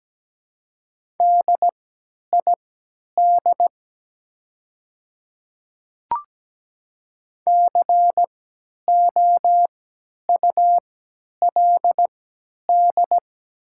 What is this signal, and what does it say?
Morse code, heard as a keyed tone, about 17 wpm, spaced as for 10 wpm, 700 Hz. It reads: DID COULD
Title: 3x Character Spacing - Top 100 Words - 17wpm